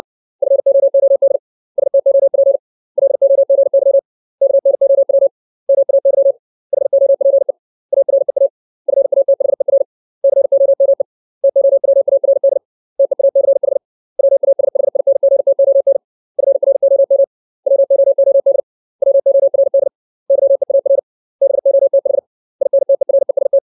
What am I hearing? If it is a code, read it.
Morse code, heard as a keyed tone, about 60 wpm, 550 Hz. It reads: FOOD STOP LOGY KNOW MAY HOPE AREA FATHER COME TOWARD TEACH WASHINGTON FROM WOOD WORD YEAR BOTH INTEREST
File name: 632 words / 632 Words 60WPM